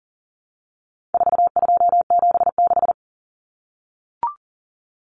Morse code, heard as a keyed tone, about 40 wpm, 700 Hz. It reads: 4276
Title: Common Cut Numbers - 4-Digit 40wpm